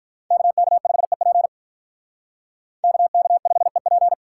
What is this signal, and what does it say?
Morse code, heard as a keyed tone, about 46 wpm, 700 Hz. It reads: KK5EP KK5EP